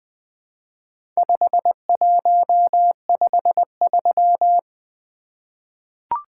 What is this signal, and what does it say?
Morse code, heard as a keyed tone, about 20 wpm, 700 Hz. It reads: 5153
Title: Four Letter-Number Combinations - 20wpm